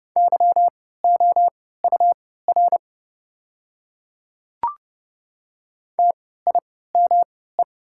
Morse code, heard as a keyed tone, about 31 wpm, spaced as for 17 wpm, 700 Hz. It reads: YOUR TIME
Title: 3x Character Spacing - Top 100 Words - 30wpm